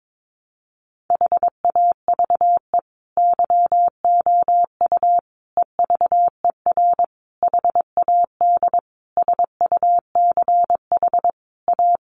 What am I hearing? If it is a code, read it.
Morse code, heard as a keyed tone, about 22 wpm, 700 Hz. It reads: HAVE YOU EVER HAD SUCH A
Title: Sentences from Top 1200 Words 22wpm